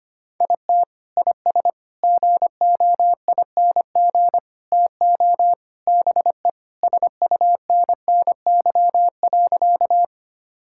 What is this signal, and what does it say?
Morse code, heard as a keyed tone, about 25 wpm, 700 Hz. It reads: IT IS GOING TO BE SUNNY.
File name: practice / 25wpm / Sentences with 4-Word Phrases - 25wpm